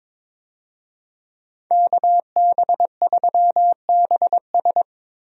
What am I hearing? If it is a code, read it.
Morse code, heard as a keyed tone, about 22 wpm, 700 Hz. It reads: KB3BS
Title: USA Call Signs - 5 Characters - Repeated 3x - 22wpm